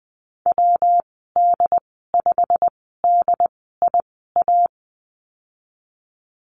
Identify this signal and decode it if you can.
Morse code, heard as a keyed tone, about 20 wpm, spaced as for 14 wpm, 700 Hz. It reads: WD5DIA
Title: 167 Lesson 27g - Number 7 - Call Signs 20wpm